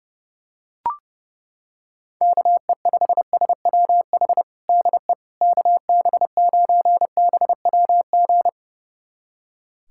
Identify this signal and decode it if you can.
Morse code, heard as a keyed tone, about 29 wpm, 700 Hz. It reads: KE5SWH DE KB9BWG
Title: USA Callsign DE Callsign - 6 Characters - 30wpm